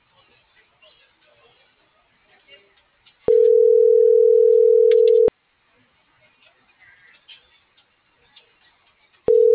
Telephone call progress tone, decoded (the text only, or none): ringback tone